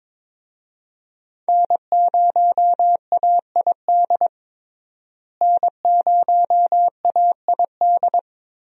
Morse code, heard as a keyed tone, about 22 wpm, 700 Hz. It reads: N0AID N0AID
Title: USA Call Signs - 5 Characters - Repeated 2x - 22wpm